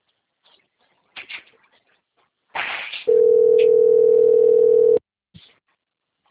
Ringback tone (North America), 440 + 480 Hz.
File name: out- - - - - .60112